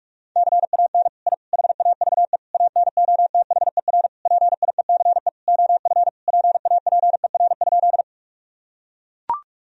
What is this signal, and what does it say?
Morse code, heard as a keyed tone, about 41 wpm, 700 Hz. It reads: CAN I HAVE ANOTHER PIECE OF PAPER?
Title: Sentences from Top 1000 Words - Review Part 2 - 45wpm